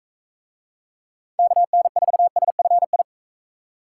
Morse code, heard as a keyed tone, about 41 wpm, 700 Hz. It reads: KN4SFI